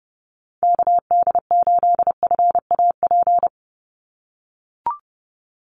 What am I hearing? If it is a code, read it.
Morse code, heard as a keyed tone, about 30 wpm, 700 Hz. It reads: KD8FAP
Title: USA Call Signs 30wpm